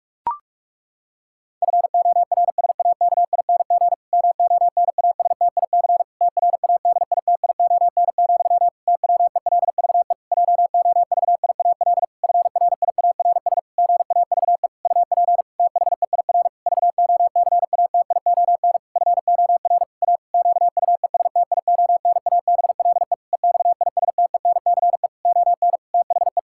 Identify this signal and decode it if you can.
Morse code, heard as a keyed tone, about 46 wpm, 700 Hz. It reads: FORSAKING MONASTIC TRADITION, TWELVE JOVIAL FRIARS GAVE UP THEIR VOCATION FOR A QUESTIONABLE EXISTENCE ON THE
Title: Pangram Challenge v2 - 45wpm